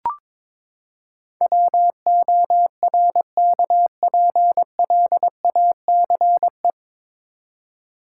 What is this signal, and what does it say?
Morse code, heard as a keyed tone, about 22 wpm, 700 Hz. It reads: WORKPLACE
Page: Morse Code Ninja practice 22wpm